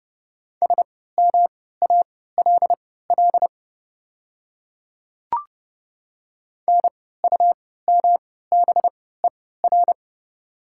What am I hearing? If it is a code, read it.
Morse code, heard as a keyed tone, about 30 wpm, spaced as for 17 wpm, 690 Hz. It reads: SMALL NUMBER
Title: 3x Character Spacing - Top 200 Words - 30wpm